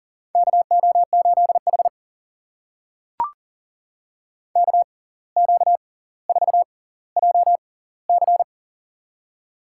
Morse code, heard as a keyed tone, about 40 wpm, spaced as for 19 wpm, 700 Hz. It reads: KO8H KQ4JC
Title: USA Call Signs - 6x Character Spacing - 40wpm